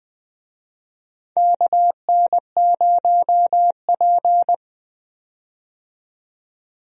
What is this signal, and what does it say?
Morse code, heard as a keyed tone, about 20 wpm, 700 Hz. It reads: KN0P
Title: USA Call Signs - 4 Characters - 20wpm